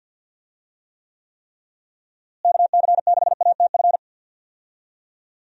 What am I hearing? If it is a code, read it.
Morse code, heard as a keyed tone, about 53 wpm, 700 Hz. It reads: KC6ATF
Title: USA Call Signs - 6 Characters - Repeated 3x - 50wpm